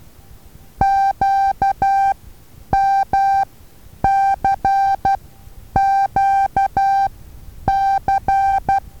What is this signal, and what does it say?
Morse code, heard as a keyed tone, about 12 wpm, spaced as for 9 wpm, 780 Hz. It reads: QMCQC
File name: Taviro HA6VA A